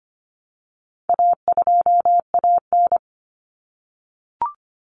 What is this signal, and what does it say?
Morse code, heard as a keyed tone, about 25 wpm, 700 Hz. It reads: A2AN
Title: Common Cut Numbers - 4-Digit 25wpm